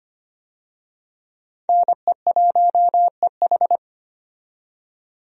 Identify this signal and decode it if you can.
Morse code, heard as a keyed tone, about 25 wpm, 700 Hz. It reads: NE1EH
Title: USA Call Signs - 5 Characters - 25wpm